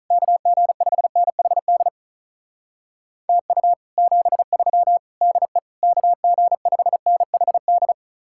Morse code, heard as a keyed tone, about 46 wpm, 700 Hz. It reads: KG5NHD TU 73 DE KG5NHD